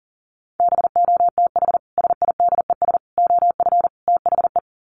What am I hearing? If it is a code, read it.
Morse code, heard as a keyed tone, about 40 wpm, 700 Hz. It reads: BOTH SIDES OF THE
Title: Sets of 4 Words 40wpm